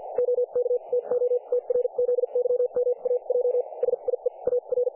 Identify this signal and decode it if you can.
Morse code, heard as a keyed tone, about 50 wpm, spaced as for 40 wpm, 500 Hz. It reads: OK TO TRZYMAJ SIE AR
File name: dlazaawansowanych / nagraniazpasma / websdr recording 3564.7kHz